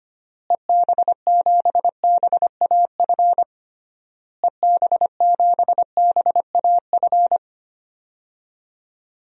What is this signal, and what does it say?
Morse code, heard as a keyed tone, about 25 wpm, 700 Hz. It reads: EB7BAF EB7BAF
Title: DXCC-100 Call Signs - 6 Characters - Repeated 3x - 25wpm